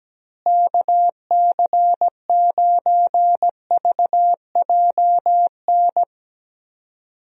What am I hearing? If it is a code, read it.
Morse code, heard as a keyed tone, about 17 wpm, 700 Hz. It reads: KC9VJN
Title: USA Call Signs - 6 Characters - Repeated 2x - 17wpm